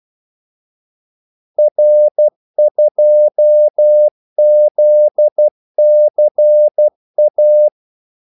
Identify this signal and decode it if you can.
Morse code, heard as a keyed tone, about 12 wpm, 600 Hz. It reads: R2ZCA
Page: 2 ekstra ordmellemrum mellem kaldesignalerne